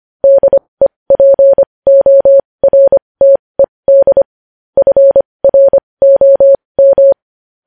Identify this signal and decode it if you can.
Morse code, heard as a keyed tone, about 25 wpm, spaced as for 20 wpm, 550 Hz. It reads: DEPORTED FROM
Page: news-headlines-in-morse-code-at-20-wpm